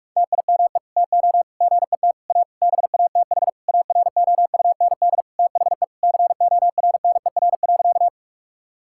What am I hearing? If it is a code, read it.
Morse code, heard as a keyed tone, about 46 wpm, 700 Hz. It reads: TIME TO GET A BATH AROUND THE CORNER.
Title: Sentences from Top 1600 Words 45wpm